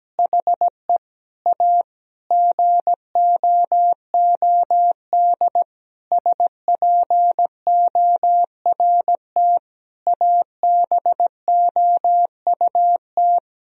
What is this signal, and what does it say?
Morse code, heard as a keyed tone, about 17 wpm, 700 Hz. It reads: HE A GOOD SPORT ABOUT
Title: Sentences from Top 1000 Words - Review Part 4 - 17wpm